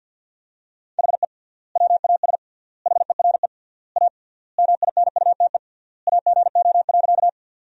Morse code, heard as a keyed tone, about 50 wpm, 700 Hz. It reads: HE WAS HERE A MINUTE AGO.
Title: 3x Word-Spacing - Sentences from Top 500 Words - 50wpm